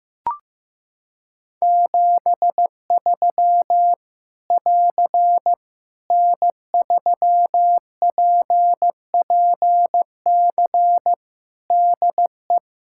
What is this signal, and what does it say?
Morse code, heard as a keyed tone, about 15 wpm, 700 Hz. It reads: 73 <AR> N3PPC DE
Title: K1USN SST QSO Practice 15wpm